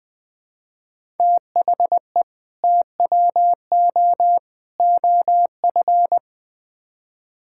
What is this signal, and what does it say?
Morse code, heard as a keyed tone, about 21 wpm, 700 Hz. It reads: THE TWO OF
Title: Build A Phrase - 4-Word Phrases - 20wpm